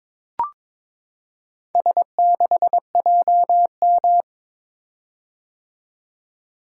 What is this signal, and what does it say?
Morse code, heard as a keyed tone, about 22 wpm, 700 Hz. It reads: S6JM
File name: practice / 22wpm / Four Letter-Number Combinations - 22wpm